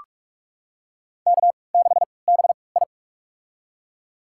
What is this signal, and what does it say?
Morse code, heard as a keyed tone, about 46 wpm, spaced as for 26 wpm, 700 Hz. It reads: K6BI
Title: USA Call Signs - 4 Characters - 3x Character Spacing - 45wpm